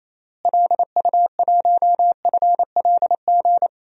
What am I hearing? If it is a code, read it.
Morse code, heard as a keyed tone, about 28 wpm, 700 Hz. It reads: LU1FLG
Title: DXCC-100 Call Signs - 6 Characters - Repeated 3x - 28wpm